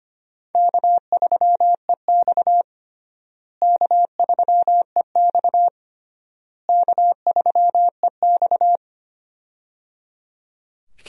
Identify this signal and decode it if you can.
Morse code, heard as a keyed tone, about 25 wpm, 700 Hz. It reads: K3EX K3EX K3EX
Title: USA Call Signs - 4 Characters - Repeated 3x - 25wpm